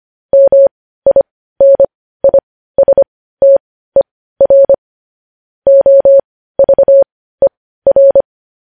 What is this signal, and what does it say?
Morse code, heard as a keyed tone, about 25 wpm, spaced as for 15 wpm, 550 Hz. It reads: MINISTER OVER
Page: News Headlines in Morse Code at 15 WPM